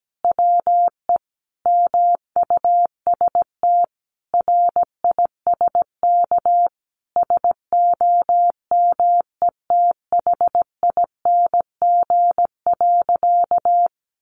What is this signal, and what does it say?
Morse code, heard as a keyed tone, about 17 wpm, 700 Hz. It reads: WE MUST RISK SOMETHING.
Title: Sentences from Top 600 Words 17wpm